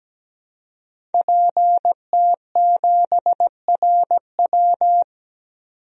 Morse code, heard as a keyed tone, about 17 wpm, 700 Hz. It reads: PT7RW
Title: DXCC-100 Call Signs - 5 Characters - 17wpm